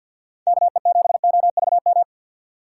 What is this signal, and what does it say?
Morse code, heard as a keyed tone, about 47 wpm, 700 Hz. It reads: KE7OVM
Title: USA Call Signs - 6 Characters - 6x Character Spacing - 50wpm